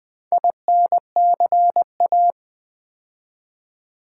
Morse code, heard as a keyed tone, about 20 wpm, 700 Hz. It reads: INCA
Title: Build A Word - 4 Letters At A Time - 11-Letter Words - 20wpm